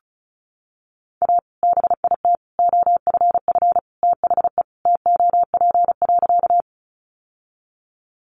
Morse code, heard as a keyed tone, about 34 wpm, 700 Hz. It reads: A BIT OFF THE TOP.
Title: Sentences from Top 900 Words 35wpm